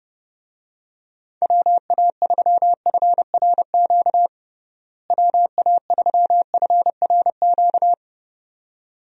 Morse code, heard as a keyed tone, about 30 wpm, 700 Hz. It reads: WA3FRQ WA3FRQ
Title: USA Call Signs - 6 Characters - Repeated 2x - 30wpm